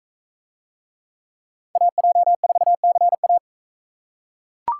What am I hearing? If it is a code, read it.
Morse code, heard as a keyed tone, about 42 wpm, 700 Hz. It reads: AJ4CA